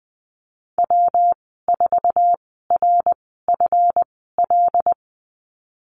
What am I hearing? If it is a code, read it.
Morse code, heard as a keyed tone, about 20 wpm, spaced as for 15 wpm, 700 Hz. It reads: W4RFL
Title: 154 Lesson 25g - Letter p - Call Signs 20wpm